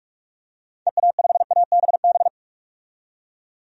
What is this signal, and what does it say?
Morse code, heard as a keyed tone, about 46 wpm, 700 Hz. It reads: EA5ABB
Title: DXCC-100 Call Signs - 6 Characters - 45wpm